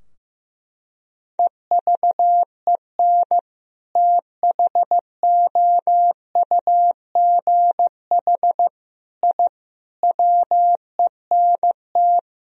Morse code, heard as a keyed tone, about 16 wpm, 700 Hz. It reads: EVEN THOUGH I WENT
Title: Sentences from Top 1000 Words - Review Part 2 - 15wpm